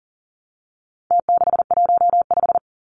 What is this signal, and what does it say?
Morse code, heard as a keyed tone, about 40 wpm, 700 Hz. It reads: T615
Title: Numbers - Contest Style - 4-Digit 40wpm